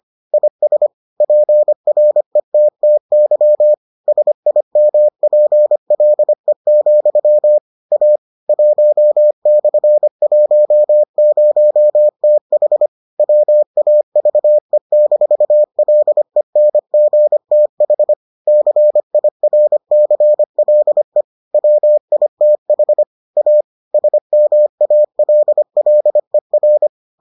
Morse code, heard as a keyed tone, about 25 wpm, 600 Hz. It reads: IS PRETTY SIMPLE, A 1/10TH WAVE-LENGTH CIRCLE WITH A SMALLER